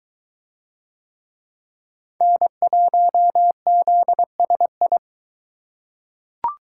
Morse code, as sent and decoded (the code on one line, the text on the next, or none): -. .---- --.. ... ..
N1ZSI